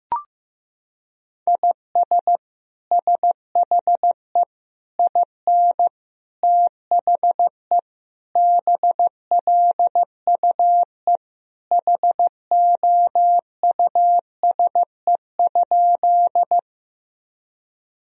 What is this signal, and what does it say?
Morse code, heard as a keyed tone, about 15 wpm, 700 Hz. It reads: IS SHE IN THE BLUE HOUSE?
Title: Sentences from Top 1000 Words - Review Part 3 - 15wpm